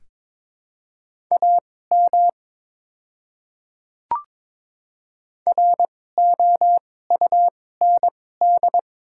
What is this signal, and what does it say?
Morse code, heard as a keyed tone, about 21 wpm, spaced as for 16 wpm, 700 Hz. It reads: AM ROUND
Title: 2x Character Spacing - Top 500 Words - 22wpm